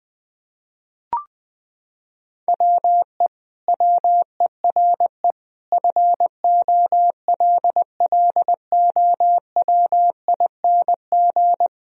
Morse code, heard as a keyed tone, about 21 wpm, 700 Hz. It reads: WE WERE FOLLOWING
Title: Sentences from Top 1000 Words - Review Part 4 - 20wpm